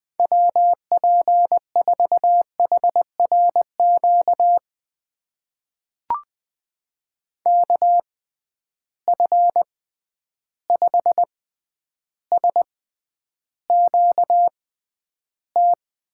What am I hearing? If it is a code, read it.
Morse code, heard as a keyed tone, about 20 wpm, spaced as for 10 wpm, 700 Hz. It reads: WP4HRQ KF5SQT
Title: USA Call Signs - 6 Characters - 6x Character Spacing - 20wpm